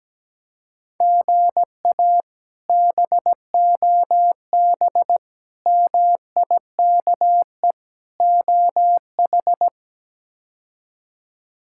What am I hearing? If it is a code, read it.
Morse code, heard as a keyed tone, about 17 wpm, 700 Hz. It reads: GA BOB MIKE OH
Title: K1USN SST QSO Practice - 17wpm